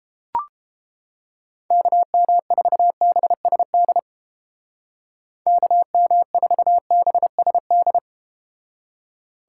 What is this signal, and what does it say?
Morse code, heard as a keyed tone, about 32 wpm, 700 Hz. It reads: KM4BSD KM4BSD